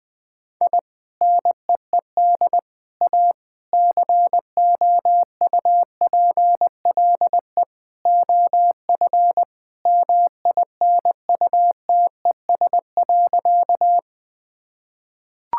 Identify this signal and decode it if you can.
Morse code, heard as a keyed tone, about 20 wpm, 700 Hz. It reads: I NEED A COUPLE OF MINUTES.
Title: Sentences with 3-Word Phrases - 20wpm